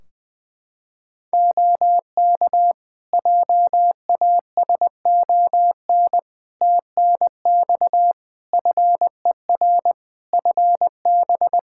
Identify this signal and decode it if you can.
Morse code, heard as a keyed tone, about 20 wpm, 700 Hz. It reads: OK JASON TNX FER FB
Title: Practice QSO - Exchange 4 - Ending - 20wpm